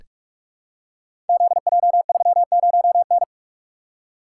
Morse code, heard as a keyed tone, about 44 wpm, 700 Hz. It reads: ZJ30N
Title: Code Groups - 5 Letter:Numbers - 45wpm